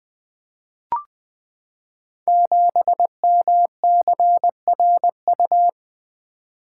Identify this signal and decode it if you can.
Morse code, heard as a keyed tone, about 20 wpm, 700 Hz. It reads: 7MCRU
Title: Code Groups - 5 Letter:Numbers - 20wpm